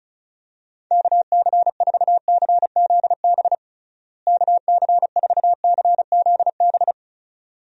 Morse code, heard as a keyed tone, about 36 wpm, 700 Hz. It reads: KC4CZB KC4CZB
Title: USA Call Signs - 6 Characters - Repeated 2x - 35wpm